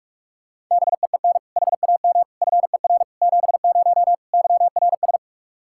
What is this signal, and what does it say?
Morse code, heard as a keyed tone, about 41 wpm, 700 Hz. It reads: BEEN HAM FER 70 YRS
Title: Practice QSO - Exchange 3 - More Info - 45wpm